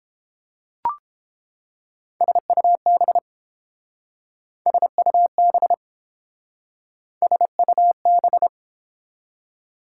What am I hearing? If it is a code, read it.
Morse code, heard as a keyed tone, about 29 wpm, 700 Hz. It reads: SUB SUB SUB